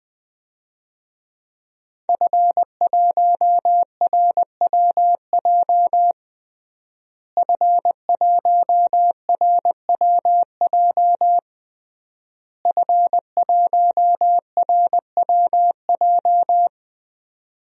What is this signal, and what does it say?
Morse code, heard as a keyed tone, about 20 wpm, 700 Hz. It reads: F1RWJ F1RWJ F1RWJ
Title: DXCC-100 Call Signs - 5 Characters - Repeated 3x - 20wpm